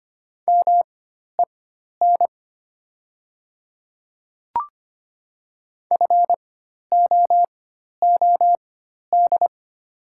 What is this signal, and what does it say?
Morse code, heard as a keyed tone, about 25 wpm, spaced as for 12 wpm, 690 Hz. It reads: MEN FOOD
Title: 4x Character Spacing - Top 300 Words - 25wpm